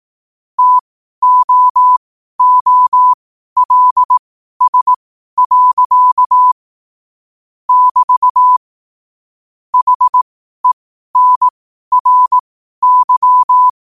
Morse code, heard as a keyed tone, about 18 wpm, spaced as for 12 wpm, 1000 Hz. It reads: TOOLS. <BT> HENRY
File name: qotdfiles / 12wpm